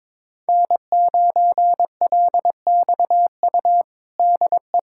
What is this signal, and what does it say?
Morse code, heard as a keyed tone, about 22 wpm, 700 Hz. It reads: N9LXU DE
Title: USA Callsign DE Callsign - 5 Characters - 22wpm